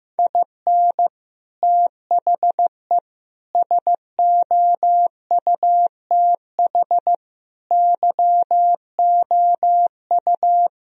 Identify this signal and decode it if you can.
Morse code, heard as a keyed tone, about 14 wpm, 700 Hz. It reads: IN THE SOUTH YOU
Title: Sentences from Top 1000 Words - Review Part 2 - 15wpm